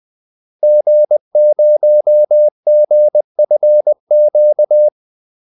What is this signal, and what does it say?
Morse code, heard as a keyed tone, about 20 wpm, 600 Hz. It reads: G0GFQ
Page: Kaldesignaler 20 wpm